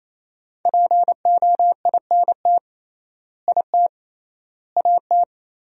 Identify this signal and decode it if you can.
Morse code, heard as a keyed tone, about 28 wpm, 700 Hz. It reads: POINT IT AT
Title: 3x Word-Spacing - Sentences from Top 300 Words - 28wpm